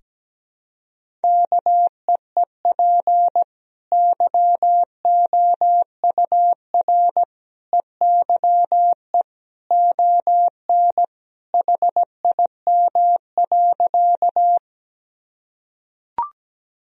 Morse code, heard as a keyed tone, about 17 wpm, 700 Hz. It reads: KEEP YOUR EYE ON HIM.
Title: Sentences from Top 1000 Words - Review Part 1 - 17wpm